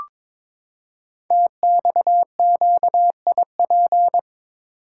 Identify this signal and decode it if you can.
Morse code, heard as a keyed tone, about 22 wpm, 700 Hz. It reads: TXQIP
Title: Code Groups - 5 Letters - 22wpm